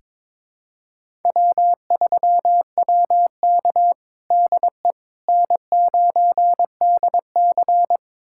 Morse code, heard as a keyed tone, about 21 wpm, 700 Hz. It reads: W3WK DE N9DC
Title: USA Callsign DE Callsign - 4 Characters - 22wpm